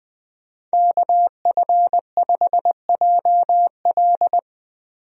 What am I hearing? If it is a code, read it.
Morse code, heard as a keyed tone, about 20 wpm, 700 Hz. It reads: KF5JL
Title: USA Call Signs - 5 Characters - 20wpm